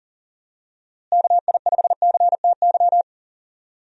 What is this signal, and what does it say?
Morse code, heard as a keyed tone, about 40 wpm, 700 Hz. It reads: KI5CTY
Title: USA Call Signs - 6 Characters - Repeated 3x - 40wpm